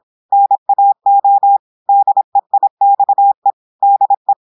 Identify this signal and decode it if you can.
Morse code, heard as a keyed tone, about 26 wpm, 800 Hz. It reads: NAO DEIXE DE
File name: portugais texte en clair 2 vitesse 26 mots minutes